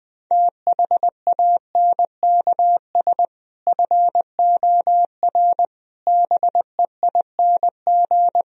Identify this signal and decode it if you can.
Morse code, heard as a keyed tone, about 19 wpm, 700 Hz. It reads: THANKS FOR BEING
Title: Speed-Racing - Sets of 5 Words 20wpm